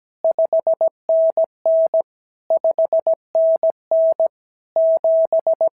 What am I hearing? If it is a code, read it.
Morse code, heard as a keyed tone, about 17 wpm, 650 Hz. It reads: 5NN 5NN 7
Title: SOTA QSO Practice 17wpm